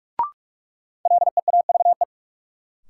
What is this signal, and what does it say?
Morse code, heard as a keyed tone, about 46 wpm, 700 Hz. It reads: LEAVE
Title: Sentences from Top 400 Words - New Word Drill - 45wpm